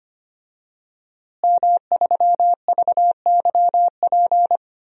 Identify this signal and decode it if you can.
Morse code, heard as a keyed tone, about 25 wpm, 700 Hz. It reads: M3VYP
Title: DXCC-100 Call Signs - 5 Characters - Repeated 3x - 25wpm